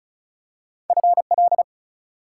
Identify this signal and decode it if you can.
Morse code, heard as a keyed tone, about 36 wpm, 700 Hz. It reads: FL
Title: US States and Canadian Provinces 35wpm